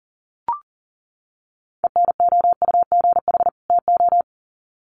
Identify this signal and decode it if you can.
Morse code, heard as a keyed tone, about 40 wpm, 700 Hz. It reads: ENOUGH TO
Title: Sets of 2 Words 40wpm